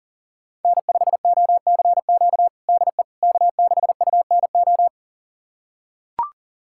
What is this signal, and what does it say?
Morse code, heard as a keyed tone, about 40 wpm, 700 Hz. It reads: N5OCQ DE K6UNO
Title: USA Callsign DE Callsign - 5 Characters - 40wpm